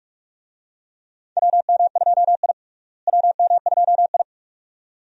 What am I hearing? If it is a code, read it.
Morse code, heard as a keyed tone, about 41 wpm, 700 Hz. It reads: WM2I WM2I
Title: USA Call Signs - 4 Characters - Repeated 2x - 45wpm